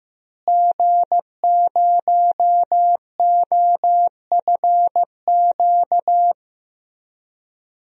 Morse code, heard as a keyed tone, about 15 wpm, 700 Hz. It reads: G0OFQ
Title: DXCC-100 Call Signs - 5 Characters - Repeated 2x - 15wpm